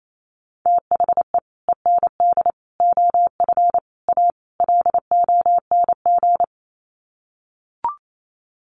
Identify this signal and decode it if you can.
Morse code, heard as a keyed tone, about 28 wpm, 700 Hz. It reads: THE END OF A LONG
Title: Sets of 5 Words Encore2 28wpm